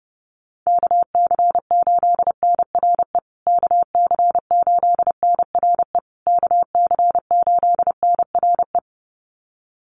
Morse code, heard as a keyed tone, about 30 wpm, 700 Hz. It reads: KC8NRE KC8NRE KC8NRE
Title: USA Call Signs Repeated 3x 30wpm